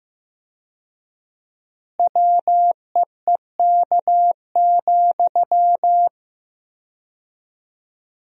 Morse code, heard as a keyed tone, about 16 wpm, 700 Hz. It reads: WEEK,
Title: 4x Word-Spacing - Sentences from Top 500 Words - 15wpm